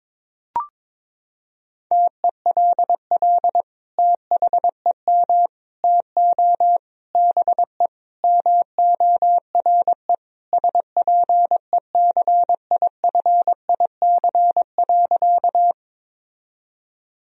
Morse code, heard as a keyed tone, about 22 wpm, 700 Hz. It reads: TELL THEM TO BE MORE SPECIFIC.
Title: Sentences from Top 1000 Words - Review Part 3 - 22wpm